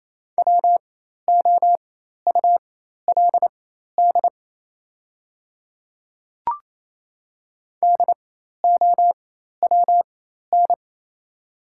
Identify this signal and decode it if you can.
Morse code, heard as a keyed tone, about 28 wpm, spaced as for 13 wpm, 700 Hz. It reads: WOULD DOWN
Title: 4x Character Spacing - Top 300 Words - 28wpm